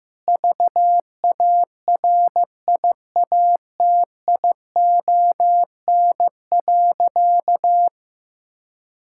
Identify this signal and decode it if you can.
Morse code, heard as a keyed tone, about 15 wpm, 700 Hz. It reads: VARIATION.
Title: Sentences from Top 1600 Words 15wpm